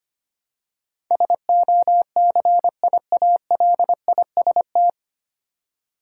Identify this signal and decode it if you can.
Morse code, heard as a keyed tone, about 25 wpm, 700 Hz. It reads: SOCIALIST
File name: practice / 25wpm / 9 Letter Words - First 3 Letters Given - 25wpm